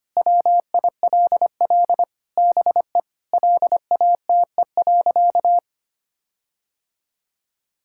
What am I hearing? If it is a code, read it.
Morse code, heard as a keyed tone, about 25 wpm, 700 Hz. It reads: WILL BE LATE.
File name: practice / 25wpm / Sentences with 4-Word Phrases - 25wpm